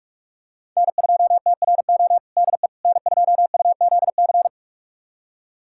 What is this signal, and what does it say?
Morse code, heard as a keyed tone, about 42 wpm, 700 Hz. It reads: N2TRO DE N2UZC
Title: USA Callsign DE Callsign - 5 Characters - 45wpm